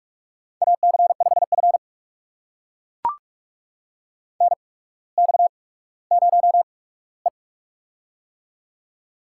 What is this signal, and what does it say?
Morse code, heard as a keyed tone, about 42 wpm, spaced as for 19 wpm, 700 Hz. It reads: AC5F NX0E
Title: USA Call Signs - 4 Characters - 8x Character Spacing - 45wpm